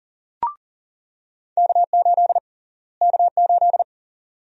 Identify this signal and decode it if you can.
Morse code, heard as a keyed tone, about 40 wpm, 700 Hz. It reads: K8 K8
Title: USA Call Sign Prefixes - Repeated 2x - 40wpm